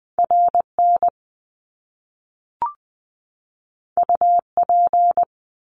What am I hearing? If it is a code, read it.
Morse code, heard as a keyed tone, about 20 wpm, 700 Hz. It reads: RN UP
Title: Top Two Letter Combinations in Words 20wpm